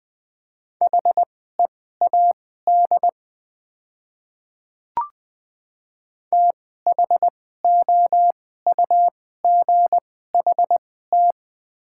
Morse code, heard as a keyed tone, about 20 wpm, spaced as for 15 wpm, 700 Hz. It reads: HEAD THOUGHT
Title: 2x Character Spacing - Top 300 Words - 20wpm